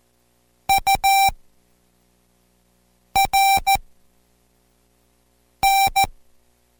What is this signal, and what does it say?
Morse code, heard as a keyed tone, about 14 wpm, spaced as for 4 wpm, 780 Hz. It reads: URN